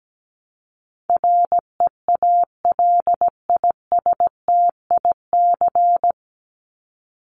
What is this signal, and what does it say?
Morse code, heard as a keyed tone, about 17 wpm, 700 Hz. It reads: REALISTIC
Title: Top 1751-2000 Words 17wpm